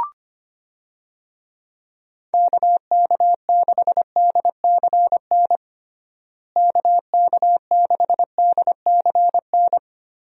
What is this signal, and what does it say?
Morse code, heard as a keyed tone, about 25 wpm, 700 Hz. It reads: KK6DCN KK6DCN
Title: USA Call Signs - 6 Characters - Repeated 3x - 25wpm